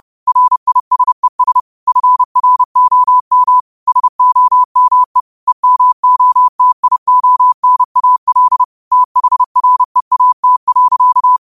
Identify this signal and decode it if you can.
Morse code, heard as a keyed tone, about 30 wpm, 1000 Hz. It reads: RISES FROM SOME EMOTIONAL THREAT.
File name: qotdfiles / 30wpm